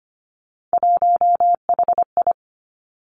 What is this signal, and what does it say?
Morse code, heard as a keyed tone, about 25 wpm, 700 Hz. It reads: 1HI
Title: Three Challenging Letter-Number Combinations 25wpm